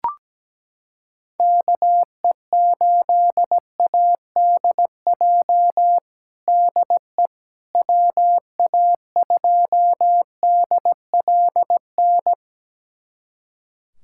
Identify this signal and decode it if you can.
Morse code, heard as a keyed tone, about 17 wpm, 700 Hz. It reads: KE8ADJ DE WA2DLN
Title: USA Callsign DE Callsign - 6 Characters - 17wpm